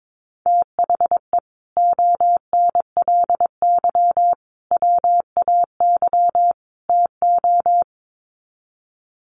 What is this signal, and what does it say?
Morse code, heard as a keyed tone, about 22 wpm, 700 Hz. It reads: THE ONLY WAY TO
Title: Sets of 4 Words 22wpm